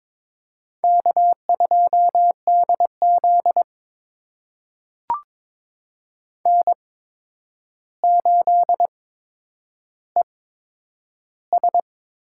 Morse code, heard as a keyed tone, about 22 wpm, spaced as for 9 wpm, 700 Hz. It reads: K2DZ N8ES
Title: USA Call Signs - 4 Characters - 8x Character Spacing - 22wpm